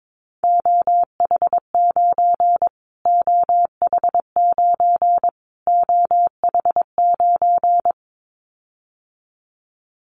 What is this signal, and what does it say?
Morse code, heard as a keyed tone, about 22 wpm, 700 Hz. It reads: OH9 OH9 OH9
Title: Callsign Prefixes for Top 25 DXCC - Repeated 3x - 22wpm